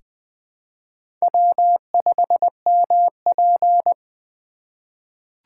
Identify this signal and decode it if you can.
Morse code, heard as a keyed tone, about 20 wpm, 700 Hz. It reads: W5MP
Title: USA Call Signs - 4 Characters - 2x Character Spacing - 20wpm